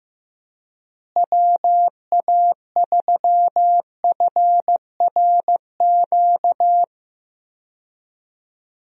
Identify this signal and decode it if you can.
Morse code, heard as a keyed tone, about 15 wpm, 700 Hz. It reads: WA3FRQ
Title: USA Call Signs - 6 Characters - Repeated 2x - 15wpm